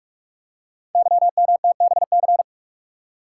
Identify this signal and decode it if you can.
Morse code, heard as a keyed tone, about 44 wpm, 690 Hz. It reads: YMTBC
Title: Code Groups - 5 Letters - 45wpm